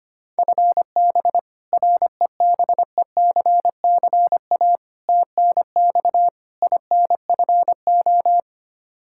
Morse code, heard as a keyed tone, about 25 wpm, 700 Hz. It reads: FB REBECCA TNX INFO
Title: Practice QSO - Exchange 3 - More Info - 25wpm